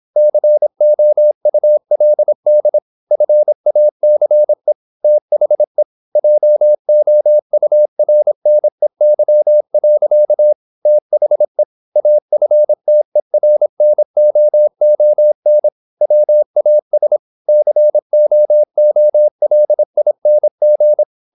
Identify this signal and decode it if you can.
Morse code, heard as a keyed tone, about 26 wpm, 600 Hz. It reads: COULD FACE THE JOURNEY. THE AFTERNOON WAS COOLING
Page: Never 26wpm | CW med Gnister